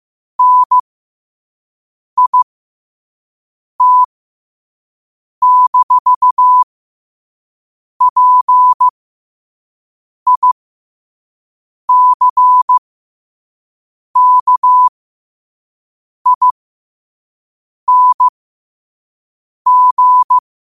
Morse code, heard as a keyed tone, about 15 wpm, spaced as for 5 wpm, 1000 Hz. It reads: NIT-PICKING
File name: qotdfiles / 5wpm